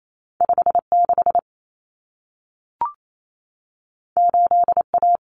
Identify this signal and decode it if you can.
Morse code, heard as a keyed tone, about 28 wpm, 700 Hz. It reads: 56 8A
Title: Common Cut Numbers - 2-Digit 28wpm